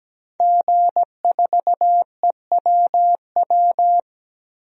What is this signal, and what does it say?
Morse code, heard as a keyed tone, about 17 wpm, 700 Hz. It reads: G4EWW
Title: DXCC-100 Call Signs - 5 Characters - Repeated 2x - 17wpm